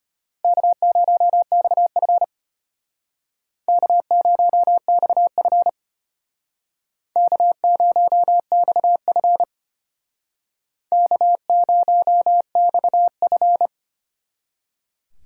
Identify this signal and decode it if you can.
Morse code, heard as a keyed tone, about 33 wpm, 700 Hz. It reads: K0XF K0XF K0XF K0XF